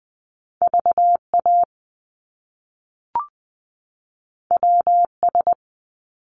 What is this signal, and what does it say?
Morse code, heard as a keyed tone, about 20 wpm, 700 Hz. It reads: VA WS
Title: Top Two Letter Combinations in Words 20wpm